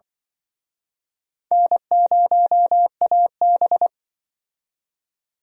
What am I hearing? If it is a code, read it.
Morse code, heard as a keyed tone, about 25 wpm, 700 Hz. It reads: N0AB
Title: Speed Racing - USA 1x2 Call Signs 20wpm